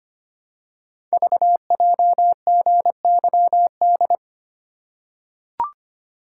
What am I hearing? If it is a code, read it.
Morse code, heard as a keyed tone, about 25 wpm, 700 Hz. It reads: VJGYD
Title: Code Groups - 5 Letters - 25wpm